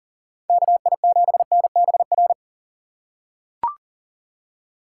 Morse code, heard as a keyed tone, about 40 wpm, 700 Hz. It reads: KI7NBR
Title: USA Call Signs - 6 Characters - 3x Character Spacing - 40wpm